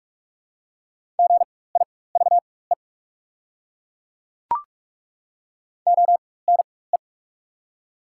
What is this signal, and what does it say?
Morse code, heard as a keyed tone, about 43 wpm, spaced as for 21 wpm, 700 Hz. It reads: GIVE ONE
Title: 4x Character Spacing - Top 500 Words - 45wpm